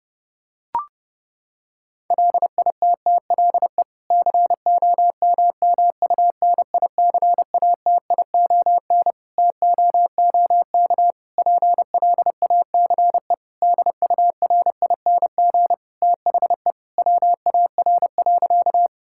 Morse code, heard as a keyed tone, about 32 wpm, 700 Hz. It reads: LITTLE COMMUNICATION TOOK PLACE DURING THE WAR.
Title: Sentences from Top 1000 Words - Review Part 6 - 30wpm